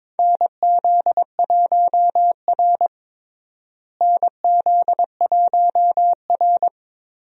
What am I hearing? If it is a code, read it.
Morse code, heard as a keyed tone, about 21 wpm, 700 Hz. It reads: NZ1R NZ1R
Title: USA Call Signs - 4 Characters - Repeated 3x - 22wpm